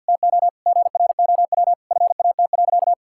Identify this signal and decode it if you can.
Morse code, heard as a keyed tone, about 50 wpm, 700 Hz. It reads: TO GROW FAT.
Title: Sentences from Top 1000 Words - Review Part 5 - 50wpm